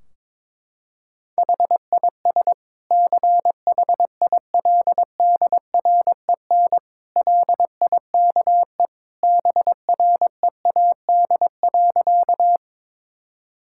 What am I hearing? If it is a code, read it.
Morse code, heard as a keyed tone, about 21 wpm, 700 Hz. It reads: HIS CHILDREN LIKE BREAD.
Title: Sentences from Top 1600 Words 22wpm